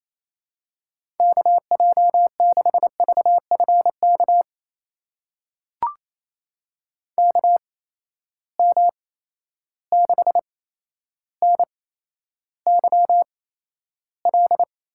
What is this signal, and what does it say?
Morse code, heard as a keyed tone, about 28 wpm, spaced as for 12 wpm, 700 Hz. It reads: KJ6VFK KM6NYL
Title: USA Call Signs - 6 Characters - 8x Character Spacing - 28wpm